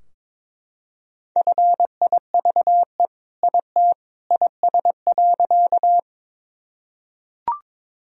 Morse code, heard as a keyed tone, about 22 wpm, 700 Hz. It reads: FIVE IT IS.
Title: Sentences from Top 1000 Words - Review Part 2 - 22wpm